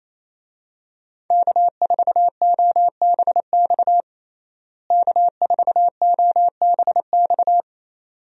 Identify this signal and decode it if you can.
Morse code, heard as a keyed tone, about 28 wpm, 700 Hz. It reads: K4OBX K4OBX
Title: USA Call Signs - 5 Characters - Repeated 2x - 28wpm